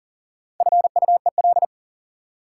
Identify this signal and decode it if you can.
Morse code, heard as a keyed tone, about 40 wpm, 700 Hz. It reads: FUEL
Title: Four Letter Words - 40wpm